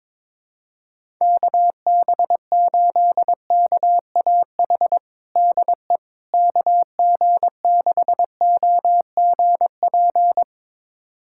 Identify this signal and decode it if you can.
Morse code, heard as a keyed tone, about 22 wpm, 700 Hz. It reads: KB8KAH DE KG6OGP
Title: USA Callsign DE Callsign - 6 Characters - 22wpm